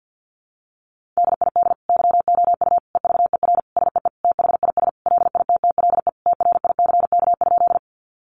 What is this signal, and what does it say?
Morse code, heard as a keyed tone, about 50 wpm, 700 Hz. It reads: DID YOU EVER SEE THIS LITTLE TRICK?
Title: Sentences from Top 1500 Words 50wpm